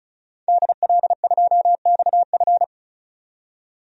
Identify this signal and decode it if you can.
Morse code, heard as a keyed tone, about 36 wpm, 700 Hz. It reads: DL2XF
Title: DXCC-100 Call Signs - 5 Characters - 35wpm